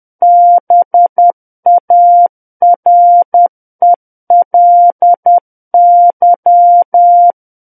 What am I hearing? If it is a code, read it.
Morse code, heard as a keyed tone, about 10 wpm, 700 Hz. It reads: BARELY
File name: Book One / 01-Chapter 1 - 10 WPM